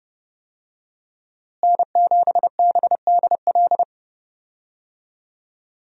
Morse code, heard as a keyed tone, about 32 wpm, 700 Hz. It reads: N7BDL
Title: USA Call Signs - 5 Characters - Repeated 3x - 30wpm